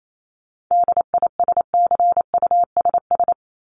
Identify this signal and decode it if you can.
Morse code, heard as a keyed tone, about 28 wpm, 700 Hz. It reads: DISCUSS
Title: Top 751-1000 Words 28wpm